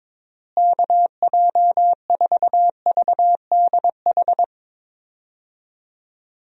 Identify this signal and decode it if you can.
Morse code, heard as a keyed tone, about 22 wpm, 700 Hz. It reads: KJ4VDH
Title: USA Call Signs - 6 Characters - Repeated 2x - 22wpm